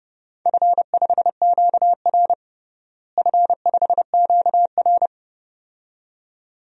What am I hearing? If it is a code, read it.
Morse code, heard as a keyed tone, about 30 wpm, 700 Hz. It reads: F5QR F5QR
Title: DXCC-100 Call Signs - 4 Characters - Repeated 2x - 30wpm